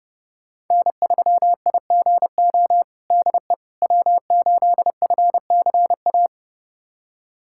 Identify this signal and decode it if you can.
Morse code, heard as a keyed tone, about 30 wpm, 700 Hz. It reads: N3IGO DE W8FCA
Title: USA Callsign DE Callsign - 5 Characters - 30wpm